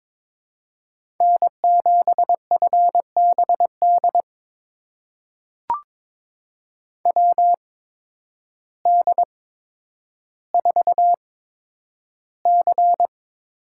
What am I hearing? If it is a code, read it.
Morse code, heard as a keyed tone, about 22 wpm, spaced as for 10 wpm, 700 Hz. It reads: N7FBD WD4C
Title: USA Call Signs - 8x Character Spacing - 22wpm